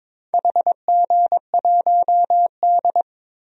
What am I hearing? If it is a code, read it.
Morse code, heard as a keyed tone, about 22 wpm, 700 Hz. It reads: HG1D
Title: DXCC-100 Call Signs - 4 Characters - 22wpm